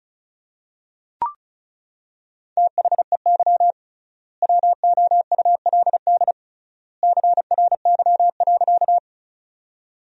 Morse code, heard as a keyed tone, about 36 wpm, 700 Hz. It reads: THEY WOULD CRY.
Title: 3x Word-Spacing - Sentences from Top 500 Words - 35wpm